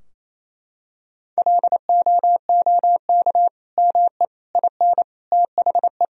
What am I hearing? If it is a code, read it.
Morse code, heard as a keyed tone, about 28 wpm, 700 Hz. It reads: LOOK ME IN THE
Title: Sentences from Top 1000 Words - Review Part 2 - 28wpm